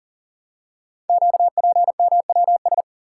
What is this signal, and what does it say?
Morse code, heard as a keyed tone, about 40 wpm, 700 Hz. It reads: QPMWS
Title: Code Groups - 5 Letters - 40wpm